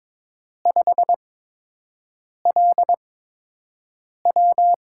Morse code, heard as a keyed tone, about 22 wpm, spaced as for 6 wpm, 700 Hz. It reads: 5LW
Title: USA Call Signs - 5 Characters - 8x Character Spacing - 22wpm